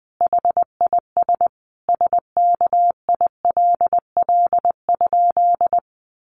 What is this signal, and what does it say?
Morse code, heard as a keyed tone, about 20 wpm, 700 Hz. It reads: HIS SKILL?
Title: Sentences from Top 1200 Words 20wpm